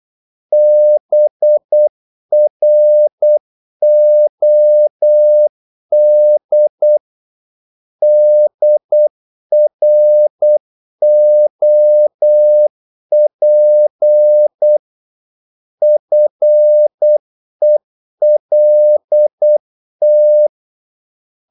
Morse code, heard as a keyed tone, about 8 wpm, 600 Hz. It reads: BROD DROP FELT
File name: Korte ord DK 8wpm